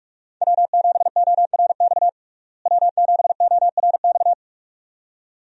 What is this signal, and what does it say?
Morse code, heard as a keyed tone, about 46 wpm, 700 Hz. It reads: W7ORX W7ORX
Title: USA Call Signs - 5 Characters - Repeated 2x - 45wpm